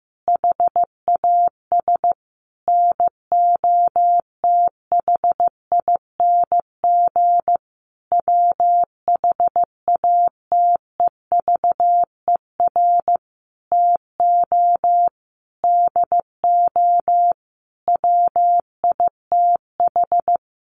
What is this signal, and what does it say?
Morse code, heard as a keyed tone, about 15 wpm, 700 Hz. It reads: HAS NOTHING WHATEVER TO DO WITH
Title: Sentences from Top 1100 Words 15wpm